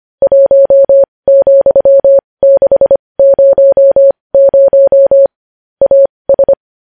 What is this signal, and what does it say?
Morse code, heard as a keyed tone, about 25 wpm, spaced as for 20 wpm, 550 Hz. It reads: 1,600 AS
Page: News Headlines in Morse Code at 20 WPM